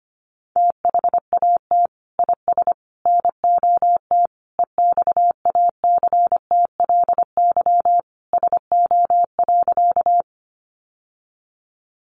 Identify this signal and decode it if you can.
Morse code, heard as a keyed tone, about 25 wpm, 700 Hz. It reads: THAT IS NOT EXACTLY SO.
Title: Sentences from Top 900 Words 25wpm